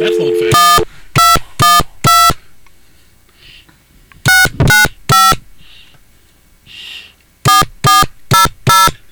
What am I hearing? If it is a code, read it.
DTMF, heard as telephone touch tones, tones of about 180 ms, with speech. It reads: 1212 695 4400